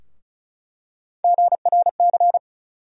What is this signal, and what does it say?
Morse code, heard as a keyed tone, about 34 wpm, 700 Hz. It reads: GRC
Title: USA Call Sign Postfixes - 3 Letters 35wpm